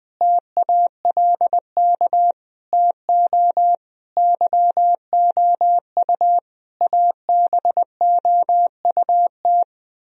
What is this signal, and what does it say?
Morse code, heard as a keyed tone, about 20 wpm, 700 Hz. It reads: TALK TO YOU ABOUT
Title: Sentences from Top 1000 Words - Review Part 1 - 20wpm